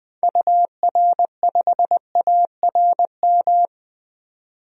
Morse code, heard as a keyed tone, about 20 wpm, 700 Hz. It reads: UR5ARM
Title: DXCC-100 Call Signs - 6 Characters - 20wpm